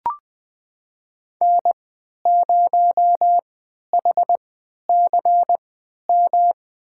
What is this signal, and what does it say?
Morse code, heard as a keyed tone, about 21 wpm, spaced as for 11 wpm, 700 Hz. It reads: N0HCM
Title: USA Call Signs - 5 Characters - 3x Character Spacing - 20wpm